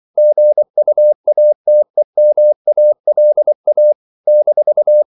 Morse code, heard as a keyed tone, about 24 wpm, 600 Hz. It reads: GUATEMALA -
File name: CW / Practice / NationalCapitals24